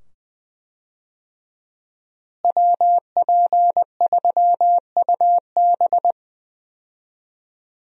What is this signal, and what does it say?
Morse code, heard as a keyed tone, about 20 wpm, 700 Hz. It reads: WP3UB
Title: USA Call Signs - 5 Characters - Repeated 2x - 20wpm